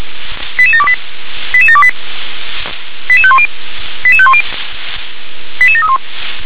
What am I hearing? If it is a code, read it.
Five-tone selective call (ZVEI), heard as quick runs of five tones, tones of about 70 ms. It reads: 88728 88728 88510 88510 88731
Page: SDR/ZVEI